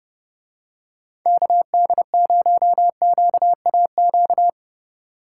Morse code, heard as a keyed tone, about 32 wpm, 700 Hz. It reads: KD0QAQ